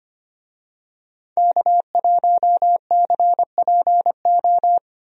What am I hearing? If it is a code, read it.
Morse code, heard as a keyed tone, about 25 wpm, 700 Hz. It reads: K1CPO
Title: USA Call Signs - 5 Characters - Repeated 2x - 25wpm